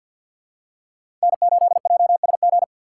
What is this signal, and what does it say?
Morse code, heard as a keyed tone, about 53 wpm, 700 Hz. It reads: N8JSG
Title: USA Call Signs - 5 Characters - 4x Character Spacing - 50wpm